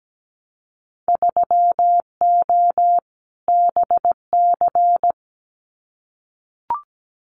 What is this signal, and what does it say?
Morse code, heard as a keyed tone, about 17 wpm, 700 Hz. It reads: 3O BC
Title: Winter Field Day Exchanges Encore 17wpm